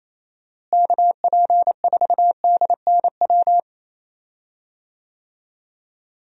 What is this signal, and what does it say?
Morse code, heard as a keyed tone, about 28 wpm, 700 Hz. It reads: KP4DNW
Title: USA Call Signs - 6 Characters - Repeated 3x - 28wpm